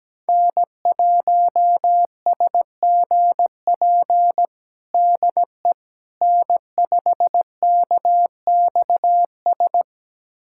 Morse code, heard as a keyed tone, about 17 wpm, 700 Hz. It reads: N1SGP DE N5KXS
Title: USA Callsign DE Callsign - 5 Characters - 17wpm